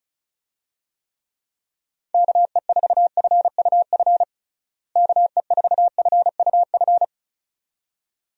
Morse code, heard as a keyed tone, about 36 wpm, 700 Hz. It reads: KE4FUF KE4FUF
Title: USA Call Signs - 6 Characters - Repeated 2x - 35wpm